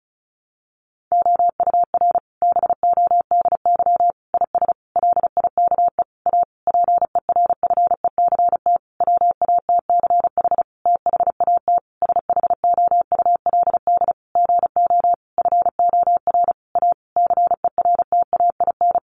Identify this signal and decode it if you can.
Morse code, heard as a keyed tone, about 34 wpm, 700 Hz. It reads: OUR BODY IS LIKE A PERFECT WATCH THAT SHOULD GO FOR A CERTAIN
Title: Sentences from Top 1000 Words 35wpm